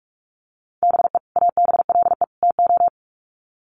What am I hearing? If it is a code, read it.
Morse code, heard as a keyed tone, about 44 wpm, 700 Hz. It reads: BE ABLE TO
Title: Sets of 3 Words 45wpm